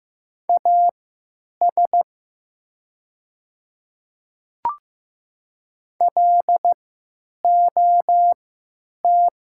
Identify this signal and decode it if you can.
Morse code, heard as a keyed tone, about 15 wpm, spaced as for 9 wpm, 700 Hz. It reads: AS LOT
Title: 3x Character Spacing - Top 500 Words - 15wpm